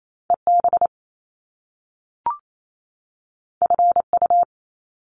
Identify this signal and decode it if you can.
Morse code, heard as a keyed tone, about 28 wpm, 700 Hz. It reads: EB FU
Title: Top Two Letter Combinations in Words 28wpm